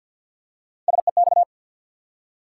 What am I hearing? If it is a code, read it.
Morse code, heard as a keyed tone, about 48 wpm, 700 Hz. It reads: SEX
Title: Three Letter Words - 50wpm